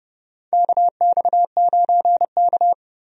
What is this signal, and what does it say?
Morse code, heard as a keyed tone, about 32 wpm, 700 Hz. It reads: KX9K
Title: USA Call Signs - 4 Characters - Repeated 2x - 30wpm